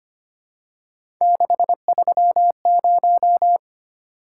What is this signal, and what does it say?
Morse code, heard as a keyed tone, about 25 wpm, 700 Hz. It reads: 630
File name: Three Letter-Number Combinations - 25wpm